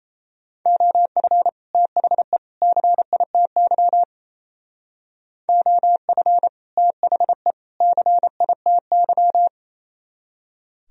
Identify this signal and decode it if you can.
Morse code, heard as a keyed tone, about 35 wpm, 700 Hz. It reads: OF THE CITY OF THE CITY
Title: Speed-Racing - Sets of 3 Words 28wpm